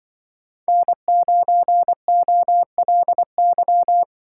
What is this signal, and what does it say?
Morse code, heard as a keyed tone, about 24 wpm, 700 Hz. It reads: N9OLY
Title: Speed Racing - USA Call Signs 20wpm